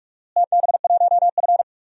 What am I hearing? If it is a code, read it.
Morse code, heard as a keyed tone, about 47 wpm, 700 Hz. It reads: TB1F
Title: DXCC-100 Call Signs - 4 Characters - Repeated 3x - 45wpm